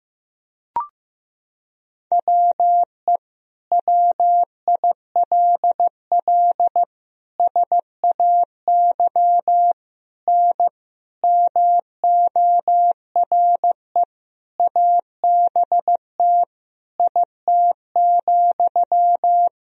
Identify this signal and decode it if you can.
Morse code, heard as a keyed tone, about 15 wpm, 700 Hz. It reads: WE WILL SAY N MORE ABT IT,
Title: Sentences from Top 200 Words and Abbreviations 15wpm